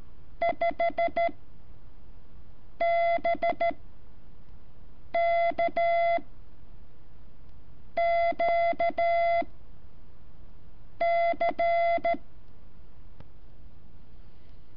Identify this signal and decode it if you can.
Morse code, heard as a keyed tone, about 13 wpm, spaced as for 5 wpm, 680 Hz. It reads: 5BKQC
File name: curso / ejercicio 17